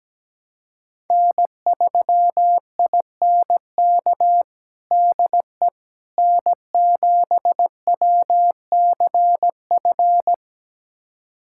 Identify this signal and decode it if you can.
Morse code, heard as a keyed tone, about 17 wpm, 700 Hz. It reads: N3INK DE N7WCF
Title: USA Callsign DE Callsign - 5 Characters - 17wpm